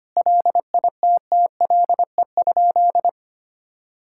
Morse code, heard as a keyed tone, about 25 wpm, 700 Hz. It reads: LITTLE?
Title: Sentences from Top 1600 Words 25wpm